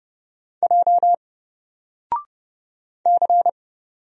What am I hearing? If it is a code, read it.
Morse code, heard as a keyed tone, about 29 wpm, 700 Hz. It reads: J C
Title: Single Character (Letter:Number:Punctuation:Prosign) 30wpm